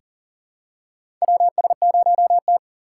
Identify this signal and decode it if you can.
Morse code, heard as a keyed tone, about 40 wpm, 700 Hz. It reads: WS0T
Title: USA Call Signs - 2x Character Spacing - 40wpm